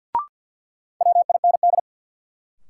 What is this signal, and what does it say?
Morse code, heard as a keyed tone, about 52 wpm, 700 Hz. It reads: WIND
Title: Sentences from Top 400 Words - New Word Drill - 50wpm